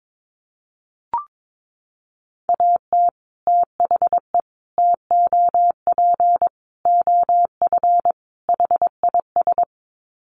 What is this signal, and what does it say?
Morse code, heard as a keyed tone, about 22 wpm, 700 Hz. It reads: AT THE TOP OF HIS
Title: Sets of 5 Words Encore 22wpm